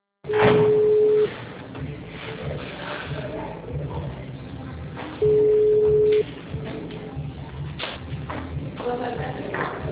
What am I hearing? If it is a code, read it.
Ringback tone (Europe, 425 Hz), with speech.
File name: out- -2005- - - .2